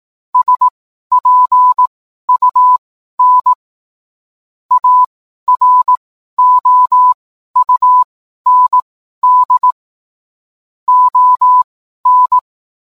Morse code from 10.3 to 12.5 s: ON